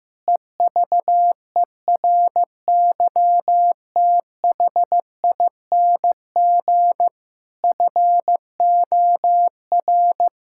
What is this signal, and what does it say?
Morse code, heard as a keyed tone, about 16 wpm, 700 Hz. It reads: EVERYTHING FOR
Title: Sentences from Top 1000 Words - Review Part 6 - 15wpm